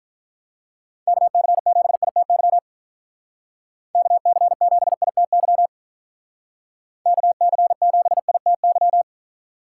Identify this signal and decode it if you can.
Morse code, heard as a keyed tone, about 44 wpm, 700 Hz. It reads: KC7ITY KC7ITY KC7ITY